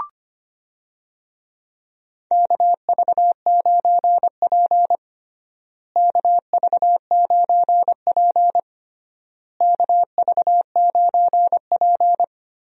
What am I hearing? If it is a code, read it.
Morse code, heard as a keyed tone, about 25 wpm, 700 Hz. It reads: KV9P KV9P KV9P
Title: USA Call Signs - 4 Characters - Repeated 3x - 25wpm